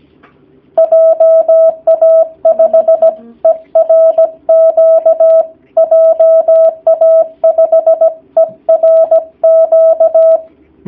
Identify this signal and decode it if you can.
Morse code, heard as a keyed tone, about 17 wpm, 650 Hz. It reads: JA5ERQ JA5ERQ
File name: jh8cbh / 2019cw / sp16- w